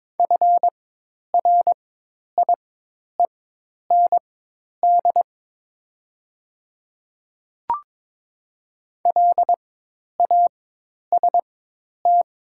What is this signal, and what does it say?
Morse code, heard as a keyed tone, about 22 wpm, spaced as for 10 wpm, 700 Hz. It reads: FRIEND LAST
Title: 4x Character Spacing - Top 300 Words - 22wpm